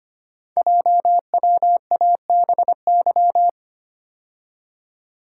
Morse code, heard as a keyed tone, about 25 wpm, 700 Hz. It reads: JWABY
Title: Code Groups - 5 Letters - 25wpm